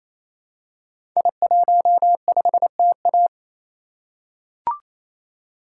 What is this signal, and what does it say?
Morse code, heard as a keyed tone, about 27 wpm, 700 Hz. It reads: I15TA
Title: Code Groups - 5 Letter:Numbers - 28wpm